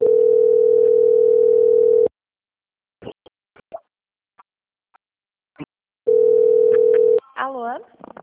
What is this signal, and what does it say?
Ringback tone (North America), 440 + 480 Hz, with speech.
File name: out- - - - - .17251